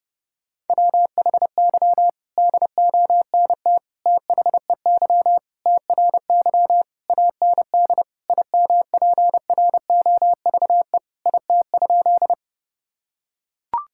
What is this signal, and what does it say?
Morse code, heard as a keyed tone, about 30 wpm, 700 Hz. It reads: WHY DONT THEY TRY AND IMPROVE IT?
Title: Sentences from Top 1000 Words - Review Part 5 - 30wpm